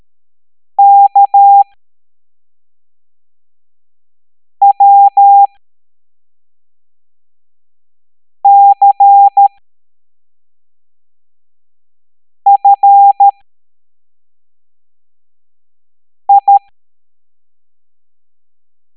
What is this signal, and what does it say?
Morse code, heard as a keyed tone, about 13 wpm, spaced as for 3 wpm, 800 Hz. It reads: KWCFI